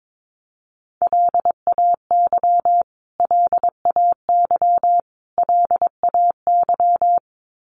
Morse code, heard as a keyed tone, about 22 wpm, 700 Hz. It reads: LAY LAY LAY
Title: Top 500 Words Repeated 22wpm